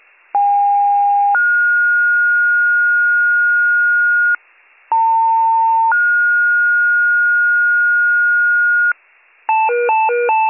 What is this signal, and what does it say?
Two-tone page, heard as a one-second tone, then a three-second tone, about 810 then 1430 Hz, about 900 then 1430 Hz. It is heard from 0.3 to 8.9 s.